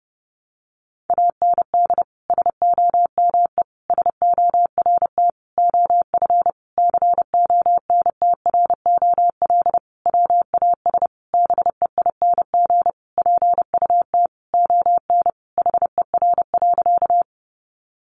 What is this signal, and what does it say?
Morse code, heard as a keyed tone, about 30 wpm, 700 Hz. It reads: AND SOME SORT OF CONTROL WAS BEING PUT ON HER.
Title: Sentences from Top 1300 Words 30wpm